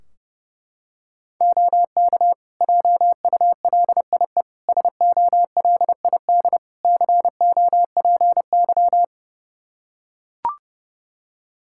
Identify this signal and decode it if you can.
Morse code, heard as a keyed tone, about 30 wpm, 700 Hz. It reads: OK JULIE SOLID COPY
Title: Practice QSO - Exchange 3 - More Info - 30wpm